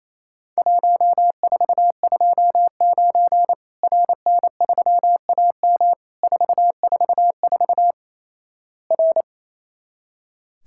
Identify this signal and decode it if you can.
Morse code, heard as a keyed tone, about 27 wpm, 700 Hz. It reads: 1429 RN3AM 444 R
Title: Worked All Europe DX Contest - QTC Exchange - 28wpm